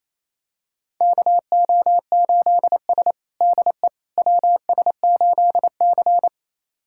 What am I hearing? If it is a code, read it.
Morse code, heard as a keyed tone, about 28 wpm, 700 Hz. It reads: KO8S DE WS8C
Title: USA Callsign DE Callsign - 4 Characters - 28wpm